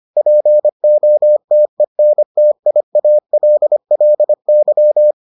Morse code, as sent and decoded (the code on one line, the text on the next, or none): .--. --- - . -. - .. .- .-.. .-.. -.--
POTENTIALLY